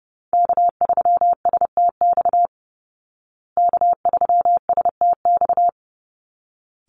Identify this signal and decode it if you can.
Morse code, heard as a keyed tone, about 30 wpm, 700 Hz. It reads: K3STX K3STX
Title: Call Signs Repeated 3x - 30wpm